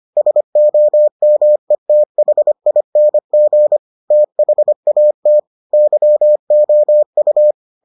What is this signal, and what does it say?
Morse code, heard as a keyed tone, about 25 wpm, 600 Hz. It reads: SOMETHING THAT YOU